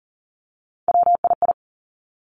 Morse code, heard as a keyed tone, about 42 wpm, 700 Hz. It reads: WII
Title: Three Challenging Letter-Number Combinations 40wpm